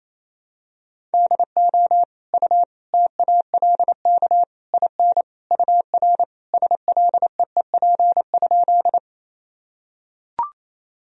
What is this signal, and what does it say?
Morse code, heard as a keyed tone, about 28 wpm, 700 Hz. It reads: DO U TALK IN UR SLEEP?
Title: Sentences from Top 500 Words and Abbreviations 28wpm